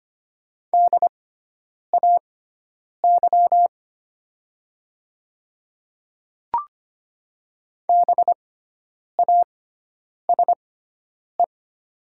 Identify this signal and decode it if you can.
Morse code, heard as a keyed tone, about 25 wpm, spaced as for 9 wpm, 700 Hz. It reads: DAY BASE
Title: 6x Character Spacing - Top 300 Words - 25wpm